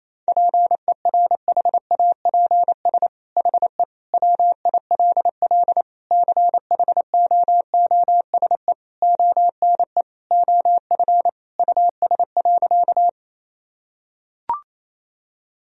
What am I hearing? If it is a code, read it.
Morse code, heard as a keyed tone, about 28 wpm, 700 Hz. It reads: PERHAPS HE WILL CHOOSE ONE OF US.
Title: Sentences from Top 1000 Words - Review Part 4 - 28wpm